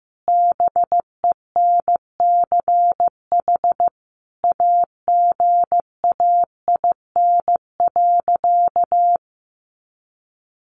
Morse code, heard as a keyed tone, about 15 wpm, 700 Hz. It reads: BENCH AGAIN.
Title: Sentences from Top 1400 Words 15wpm